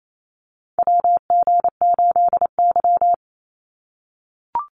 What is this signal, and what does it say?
Morse code, heard as a keyed tone, about 28 wpm, 700 Hz. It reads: WG8Y
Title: NA SOTA Chaser Callsigns 28wpm